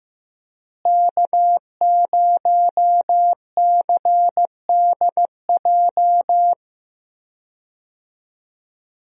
Morse code, heard as a keyed tone, about 15 wpm, 700 Hz. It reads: K0CDJ
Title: USA Call Signs - 5 Characters - Repeated 2x - 15wpm